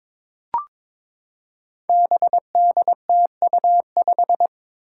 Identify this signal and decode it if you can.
Morse code, heard as a keyed tone, about 22 wpm, 700 Hz. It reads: BDTU5
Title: Five Challenging Letter-Number Combinations 22wpm